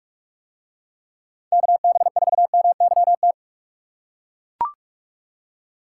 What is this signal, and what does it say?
Morse code, heard as a keyed tone, about 40 wpm, 700 Hz. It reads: KB4MYT